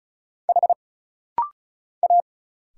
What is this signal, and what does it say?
Morse code, heard as a keyed tone, about 36 wpm, 700 Hz. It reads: H A
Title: Single Letter-Number Alt - Rapid-Fire 35wpm